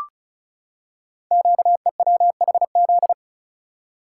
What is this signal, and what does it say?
Morse code, heard as a keyed tone, about 36 wpm, 700 Hz. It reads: QEWHZ
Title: Code Groups - 5 Letter:Numbers - 35wpm